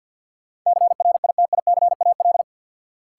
Morse code, heard as a keyed tone, about 52 wpm, 700 Hz. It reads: CRITICAL
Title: Sentences from Top 1100 Words - New Word Drill - 50wpm